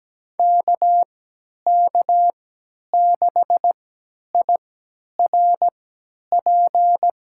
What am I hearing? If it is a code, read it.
Morse code, heard as a keyed tone, about 17 wpm, spaced as for 10 wpm, 700 Hz. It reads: KK6IRP
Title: USA Call Signs - 3x Character Spacing - 17wpm